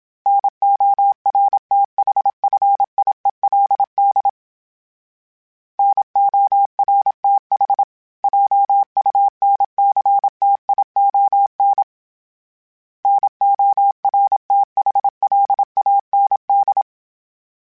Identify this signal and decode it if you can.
Morse code, heard as a keyed tone, about 27 wpm, 800 Hz. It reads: NORTHFIELD NORTH JUNCTION NORTHLAND
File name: fromSMs / FromSMs mp3s / Cities@25